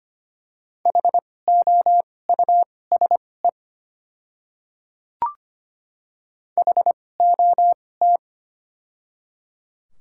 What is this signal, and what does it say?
Morse code, heard as a keyed tone, about 25 wpm, spaced as for 18 wpm, 690 Hz. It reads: HOUSE HOT
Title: 2x Character Spacing - Top 300 Words - 25wpm